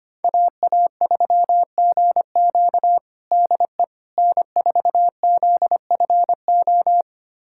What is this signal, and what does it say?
Morse code, heard as a keyed tone, about 25 wpm, 700 Hz. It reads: AA3GQ DE N4ZFO
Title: USA Callsign DE Callsign - 5 Characters - 25wpm